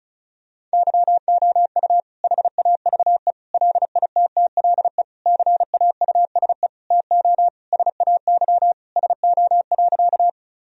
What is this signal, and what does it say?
Morse code, heard as a keyed tone, about 36 wpm, 700 Hz. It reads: YOU HAVE LITTLE CAUSE TO SAY SO.
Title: Sentences from Top 1000 Words - Review Part 1 - 35wpm